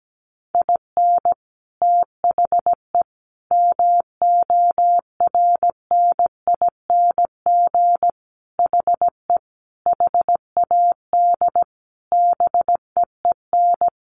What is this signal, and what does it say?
Morse code, heard as a keyed tone, about 17 wpm, 700 Hz. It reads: IN THE MORNING HE HAD BEEN
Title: Sentences from Top 1200 Words 17wpm